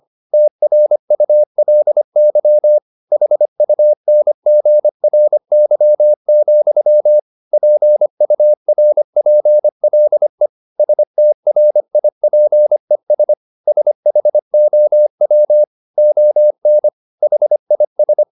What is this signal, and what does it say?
Morse code, heard as a keyed tone, about 25 wpm, 600 Hz. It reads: TRULY HUNGRY, PURPLE STRIPES SHOW ON HIS